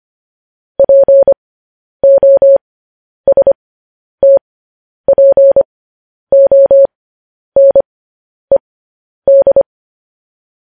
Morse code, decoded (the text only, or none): POSTPONED